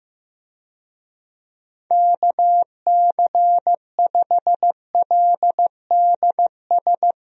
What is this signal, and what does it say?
Morse code, heard as a keyed tone, about 15 wpm, 700 Hz. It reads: KC5LDS
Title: USA Call Signs - 6 Characters - Repeated 3x - 15wpm